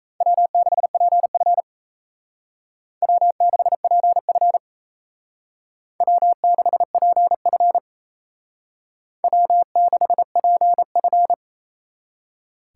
Morse code, heard as a keyed tone, about 36 wpm, 700 Hz. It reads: W6PF W6PF W6PF W6PF